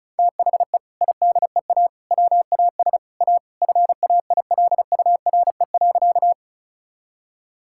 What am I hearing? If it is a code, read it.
Morse code, heard as a keyed tone, about 34 wpm, 700 Hz. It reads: THE IDEA WAS A FAILURE.
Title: Sentences from Top 1000 Words - Review Part 6 - 35wpm